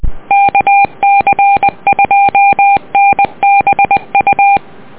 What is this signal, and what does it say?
Morse code, heard as a keyed tone, about 20 wpm, 800 Hz. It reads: KC2NBU
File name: net audio